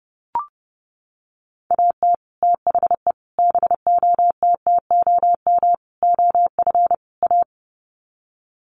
Morse code, heard as a keyed tone, about 28 wpm, 700 Hz. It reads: AT THE BOTTOM OF A
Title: Sets of 5 Words Encore2 30wpm